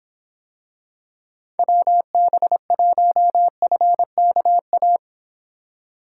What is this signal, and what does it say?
Morse code, heard as a keyed tone, about 25 wpm, 700 Hz. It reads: WB1FKA